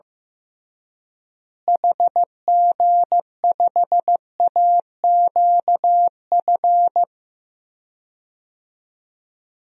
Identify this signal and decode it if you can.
Morse code, heard as a keyed tone, about 15 wpm, 700 Hz. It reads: HG5AQF
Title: DXCC-100 Call Signs - 6 Characters - Repeated 3x - 15wpm